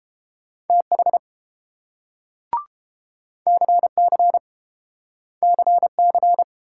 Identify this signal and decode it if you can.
Morse code, heard as a keyed tone, about 30 wpm, 700 Hz. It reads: TH CC CC